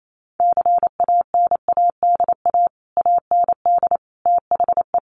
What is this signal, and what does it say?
Morse code, heard as a keyed tone, about 28 wpm, 700 Hz. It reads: CANADA AND THE
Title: Sets of 5 Words Encore 28wpm